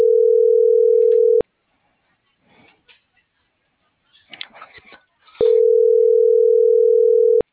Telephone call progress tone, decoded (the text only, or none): ringback tone